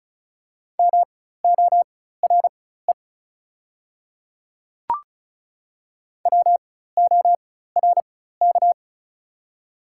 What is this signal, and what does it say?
Morse code, heard as a keyed tone, about 36 wpm, spaced as for 16 wpm, 700 Hz. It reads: MORE WORK
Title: 4x Character Spacing - Top 200 Words - 35wpm